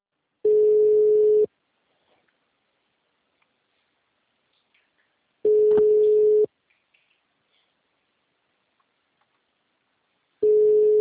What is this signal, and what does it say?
Ringback tone (Europe, 425 Hz).
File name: out- -2028- - - .30067